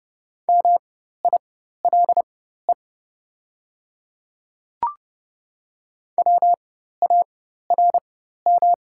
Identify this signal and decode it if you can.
Morse code, heard as a keyed tone, about 30 wpm, spaced as for 14 wpm, 700 Hz. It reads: MILE WARM
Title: 4x Character Spacing - Top 500 Words - 30wpm